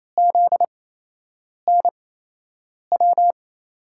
Morse code, heard as a keyed tone, about 28 wpm, spaced as for 8 wpm, 700 Hz. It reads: ZNW
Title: USA Call Signs - 8x Character Spacing - 28wpm